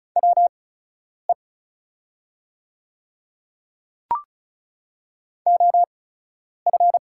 Morse code, heard as a keyed tone, about 34 wpm, spaced as for 10 wpm, 700 Hz. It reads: WE OF
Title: 8x Character Spacing - Top 100 Words - 35wpm